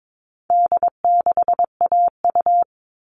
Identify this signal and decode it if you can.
Morse code, heard as a keyed tone, about 22 wpm, 700 Hz. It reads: D6AU
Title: All Cut Numbers - 4-Digit 22wpm